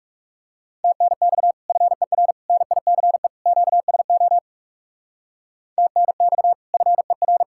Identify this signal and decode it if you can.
Morse code, heard as a keyed tone, about 37 wpm, 700 Hz. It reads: TNX FER NICE QSO TNX FER
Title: Speed-Racing Common QSO Phrases Encore - 30wpm